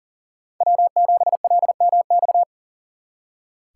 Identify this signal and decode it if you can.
Morse code, heard as a keyed tone, about 52 wpm, 700 Hz. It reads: W7LMX